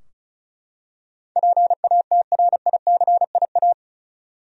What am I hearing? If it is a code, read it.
Morse code, heard as a keyed tone, about 36 wpm, 700 Hz. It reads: PATRICIA
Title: USA Common Names - 35wpm